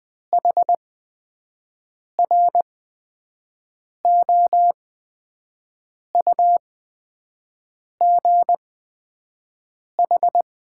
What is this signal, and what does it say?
Morse code, heard as a keyed tone, about 20 wpm, spaced as for 5 wpm, 700 Hz. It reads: HROUGH
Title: 8x Character Spacing - Top 200 Words - 20wpm